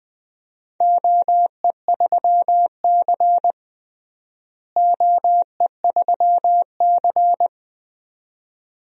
Morse code, heard as a keyed tone, about 20 wpm, 700 Hz. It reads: OE3C OE3C
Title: DXCC-100 Call Signs - 4 Characters - Repeated 2x - 20wpm